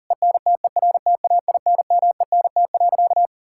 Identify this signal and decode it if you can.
Morse code, heard as a keyed tone, about 40 wpm, 690 Hz. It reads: ENTERTAINMENT.
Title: Sentences from Top 1000 Words - Review Part 6 - 40wpm